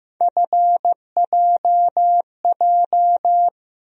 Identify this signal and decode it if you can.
Morse code, heard as a keyed tone, about 15 wpm, 700 Hz. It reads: FJJ
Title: USA Call Sign Postfixes - 3 Letters 15wpm